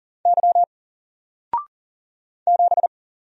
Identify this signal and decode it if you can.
Morse code, heard as a keyed tone, about 40 wpm, 700 Hz. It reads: Y 7
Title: Single Letter-Number 40wpm